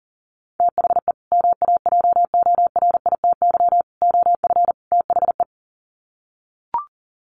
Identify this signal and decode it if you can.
Morse code, heard as a keyed tone, about 40 wpm, 700 Hz. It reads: THE MAJORITY OF THE
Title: Sets of 4 Words 40wpm